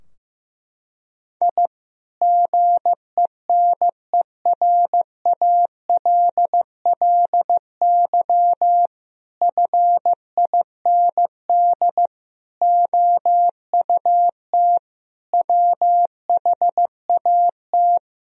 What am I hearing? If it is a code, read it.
Morse code, heard as a keyed tone, about 15 wpm, 700 Hz. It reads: I GENERALLY FIND OUT WHAT
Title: Sentences from Top 1000 Words - Review Part 4 - 15wpm